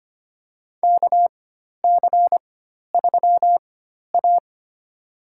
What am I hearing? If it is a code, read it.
Morse code, heard as a keyed tone, about 25 wpm, spaced as for 12 wpm, 700 Hz. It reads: KC3A
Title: USA Call Signs - 4x Character Spacing - 25wpm